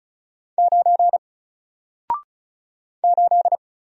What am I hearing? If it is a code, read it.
Morse code, heard as a keyed tone, about 35 wpm, 700 Hz. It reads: 9 8
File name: Single-Letter-Number-35wpm